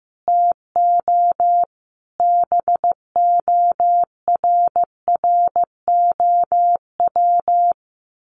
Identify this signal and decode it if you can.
Morse code, heard as a keyed tone, about 15 wpm, 700 Hz. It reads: TO BORROW
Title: Sentences from Top 1500 Words 15wpm